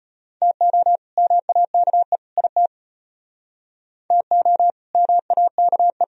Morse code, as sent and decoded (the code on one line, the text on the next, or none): - --- / -- .- -.- . / .. - / - --- / -- .- -.- .
TO MAKE IT TO MAKE